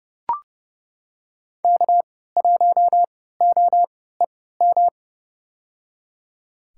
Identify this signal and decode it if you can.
Morse code, heard as a keyed tone, about 32 wpm, spaced as for 17 wpm, 700 Hz. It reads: K1OEM
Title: USA Call Signs - 5 Characters - 3x Character Spacing - 30wpm